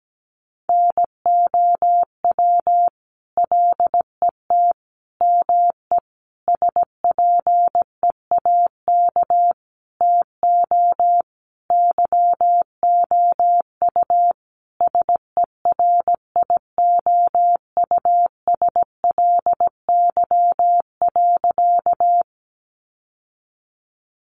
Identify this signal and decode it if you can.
Morse code, heard as a keyed tone, about 17 wpm, 700 Hz. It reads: NOW LET ME SPEAK TO YOU SERIOUSLY.
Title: Sentences from Top 1200 Words 17wpm